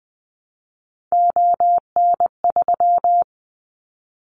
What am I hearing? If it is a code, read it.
Morse code, heard as a keyed tone, about 20 wpm, 700 Hz. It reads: ON3
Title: Callsign Prefixes for Top 25 DXCC - Repeated 3x - 20wpm